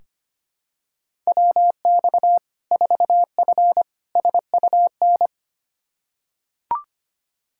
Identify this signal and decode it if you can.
Morse code, heard as a keyed tone, about 25 wpm, 700 Hz. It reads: WX 4F SUN
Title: Practice QSO - Exchange 2 - Info - 25wpm